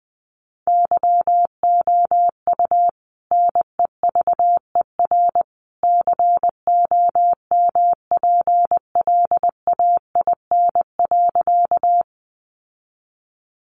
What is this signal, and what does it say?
Morse code, heard as a keyed tone, about 20 wpm, 700 Hz. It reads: YOU NEVER COMPLAIN.
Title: Sentences from Top 1300 Words 20wpm